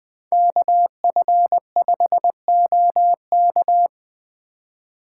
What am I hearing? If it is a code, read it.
Morse code, heard as a keyed tone, about 20 wpm, 700 Hz. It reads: KF5OK
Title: Speed Racing - USA 2x2 Call Signs 17wpm